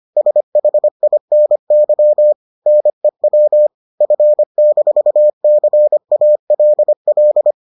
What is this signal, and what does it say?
Morse code, heard as a keyed tone, about 25 wpm, 600 Hz. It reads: SHINY NEW F-CALL